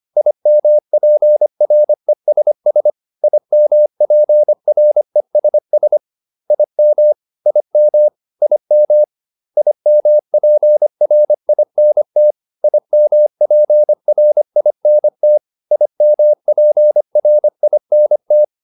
Morse code, heard as a keyed tone, about 25 wpm, 600 Hz. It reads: IMPRESS IMPRESS IM IM IM IMPRINT IMPRINT IMPRINT